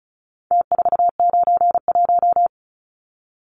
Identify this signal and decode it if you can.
Morse code, heard as a keyed tone, about 34 wpm, 700 Hz. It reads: T491
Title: Numbers - Contest Style - 4-Digit 35wpm